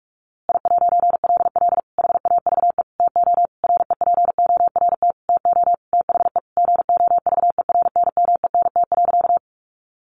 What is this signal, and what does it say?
Morse code, heard as a keyed tone, about 44 wpm, 700 Hz. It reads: I'LL HAVE TO REPORT TO THE GOVERNMENT.
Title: A Sound Of Thunder 45wpm